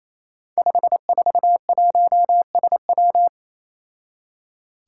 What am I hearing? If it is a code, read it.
Morse code, heard as a keyed tone, about 29 wpm, 700 Hz. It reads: 541SW
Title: Five Challenging Letter-Number Combinations 28wpm